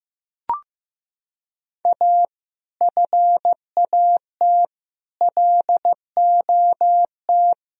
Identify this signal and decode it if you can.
Morse code, heard as a keyed tone, about 15 wpm, 700 Hz. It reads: A FAT LOT
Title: Sentences from Top 1000 Words - Review Part 4 - 15wpm